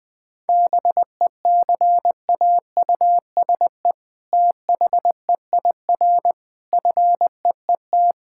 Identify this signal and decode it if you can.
Morse code, heard as a keyed tone, about 20 wpm, 700 Hz. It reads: BECAUSE THEIR FEET
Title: All-Age Appropriate Jokes - 20wpm